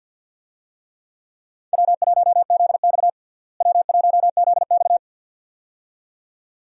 Morse code, heard as a keyed tone, about 47 wpm, 700 Hz. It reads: W1ZX W1ZX
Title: USA Call Signs - 4 Characters - Repeated 2x - 50wpm